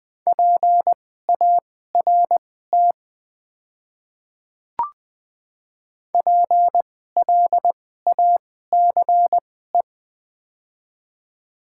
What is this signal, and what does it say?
Morse code, heard as a keyed tone, about 20 wpm, spaced as for 15 wpm, 700 Hz. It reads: PART PLACE
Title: 2x Character Spacing - Top 300 Words - 20wpm